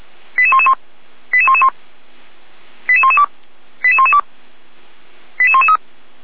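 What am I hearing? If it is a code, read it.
Five-tone selective call (ZVEI), heard as quick runs of five tones, tones of about 70 ms. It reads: 88111 88111 88112 88112 88113
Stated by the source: ZVEI five-tone selective call received on an SDR, fire brigade Orlamünde (feuerwehr-orlamuende).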